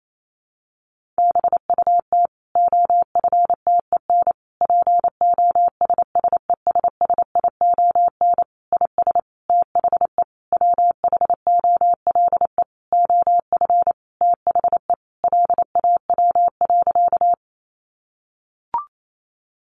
Morse code, heard as a keyed tone, about 28 wpm, 700 Hz. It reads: BUT OFTEN POSSESSION IS THE WHOLE OF THE LAW.
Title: Sentences from Top 1400 Words 28wpm